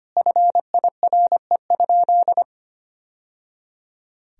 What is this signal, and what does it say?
Morse code, heard as a keyed tone, about 25 wpm, 700 Hz. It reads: FIRE?
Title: Sentences from Top 1000 Words - Review Part 1 - 25wpm